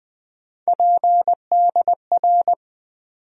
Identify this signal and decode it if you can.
Morse code, heard as a keyed tone, about 20 wpm, 700 Hz. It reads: PDR
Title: USA Call Sign Postfixes - 3 Letters 20wpm